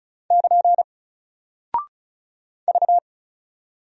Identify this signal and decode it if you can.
Morse code, heard as a keyed tone, about 36 wpm, 700 Hz. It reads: <KN> V
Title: Single Character (Letter:Number:Punctuation:Prosign) Alt 35wpm